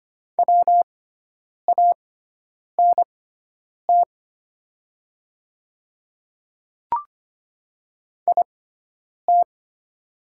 Morse code, heard as a keyed tone, about 25 wpm, spaced as for 9 wpm, 690 Hz. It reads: WANT IT
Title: 6x Character Spacing - Top 200 Words - 25wpm